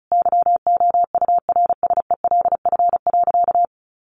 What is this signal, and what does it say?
Morse code, heard as a keyed tone, about 34 wpm, 700 Hz. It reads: YOURSELF.
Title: Sentences from Top 1500 Words 35wpm